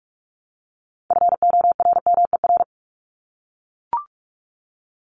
Morse code, heard as a keyed tone, about 44 wpm, 700 Hz. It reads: FORMER
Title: Top 1751-2000 Words 45wpm